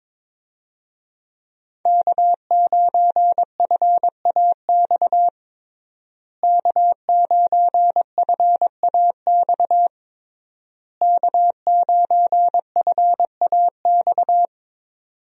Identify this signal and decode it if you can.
Morse code, heard as a keyed tone, about 22 wpm, 700 Hz. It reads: K9FAX K9FAX K9FAX
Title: USA Call Signs - 5 Characters - Repeated 3x - 22wpm